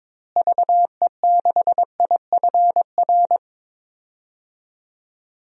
Morse code, heard as a keyed tone, about 22 wpm, 700 Hz. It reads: VE6IFR
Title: DXCC-100 Call Signs - 6 Characters - Repeated 3x - 22wpm